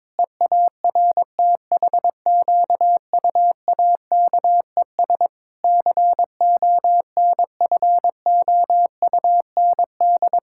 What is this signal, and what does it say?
Morse code, heard as a keyed tone, about 22 wpm, 700 Hz. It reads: EARTHQUAKES CONFOUND
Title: Pangram Challenge v2 - 22wpm